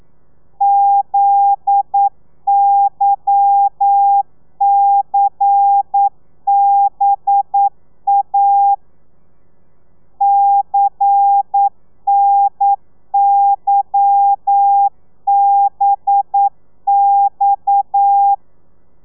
Morse code, heard as a keyed tone, about 9 wpm, 800 Hz. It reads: ZYCBA CNYBX